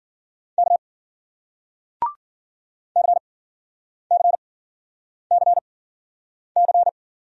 Morse code, heard as a keyed tone, about 54 wpm, 700 Hz. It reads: K C C C C